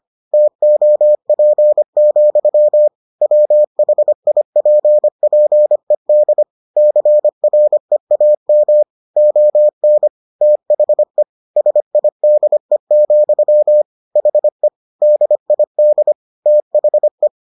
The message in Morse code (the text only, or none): TOP, WHIPPED CREAM ON THE SIDE, HE DID THE